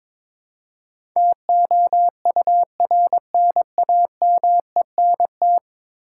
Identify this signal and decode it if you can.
Morse code, heard as a keyed tone, about 22 wpm, 700 Hz. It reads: TOURNAMENT
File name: practice / 22wpm / 10 Letter Words - First 3 Letters Given - 22wpm